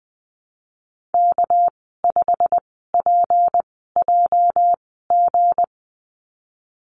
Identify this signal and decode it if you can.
Morse code, heard as a keyed tone, about 19 wpm, spaced as for 14 wpm, 700 Hz. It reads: K5PJG
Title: 213 Lesson 34g - Number 8 - Call Signs 20wpm